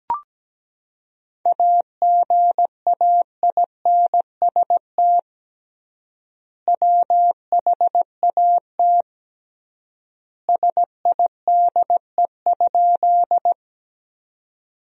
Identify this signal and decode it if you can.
Morse code, heard as a keyed tone, about 17 wpm, 700 Hz. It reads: AGAINST WHAT SIDE?
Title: 3x Word-Spacing - Sentences from Top 400 Words - 17wpm